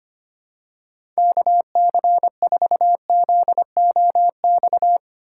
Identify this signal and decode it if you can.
Morse code, heard as a keyed tone, about 25 wpm, 700 Hz. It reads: KC4ZOX
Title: USA Call Signs - 6x Character Spacing - 25wpm